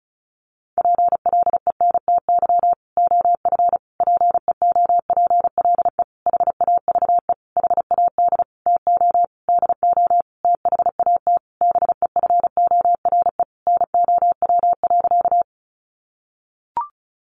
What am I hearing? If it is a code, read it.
Morse code, heard as a keyed tone, about 34 wpm, 700 Hz. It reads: PLENTY OF PEOPLE HAVE HAD TO DO THAT BEFORE NOW.
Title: Sentences from Top 1300 Words 35wpm